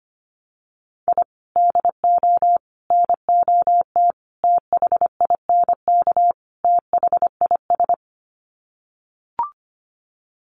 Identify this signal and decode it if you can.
Morse code, heard as a keyed tone, about 25 wpm, 700 Hz. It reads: I DO NOT THINK THIS
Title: Sets of 5 Words Encore 25wpm